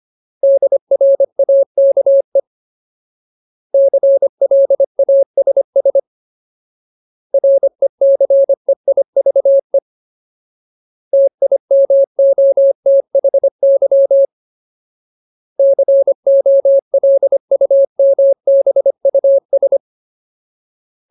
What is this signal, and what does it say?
Morse code, heard as a keyed tone, about 25 wpm, 550 Hz. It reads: DRAKE CLASS RECEIVE TIMOTHY COLUMBUS
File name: Ultimate Word List / 25WPM Ultimate